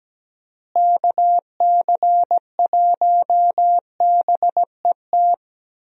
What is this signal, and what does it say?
Morse code, heard as a keyed tone, about 17 wpm, 700 Hz. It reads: KC1BET
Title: USA Call Signs - 6 Characters - Repeated 2x - 17wpm